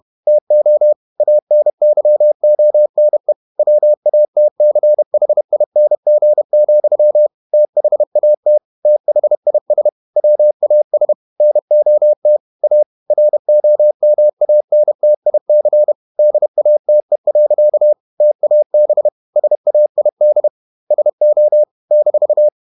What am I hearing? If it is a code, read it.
Morse code, heard as a keyed tone, about 31 wpm, 600 Hz. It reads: TO ANYONE WATCHING, THAT THIS WAS NOT A ROMANTIC DATE. TAB SAID SO -